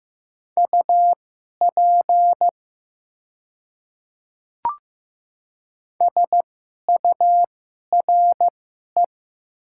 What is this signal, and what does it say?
Morse code, heard as a keyed tone, about 15 wpm, spaced as for 11 wpm, 700 Hz. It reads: UP SURE
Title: 2x Character Spacing - Top 300 Words - 15wpm